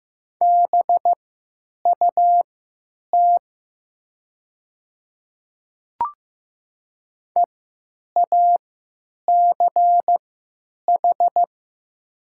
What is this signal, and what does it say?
Morse code, heard as a keyed tone, about 15 wpm, spaced as for 9 wpm, 700 Hz. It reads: BUT EACH
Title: 3x Character Spacing - Top 200 Words - 15wpm